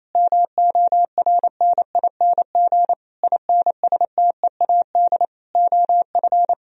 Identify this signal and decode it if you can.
Morse code, heard as a keyed tone, about 28 wpm, 700 Hz. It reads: MORNING INSTEAD OF
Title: Sentences from Top 1000 Words - Review Part 4 - 28wpm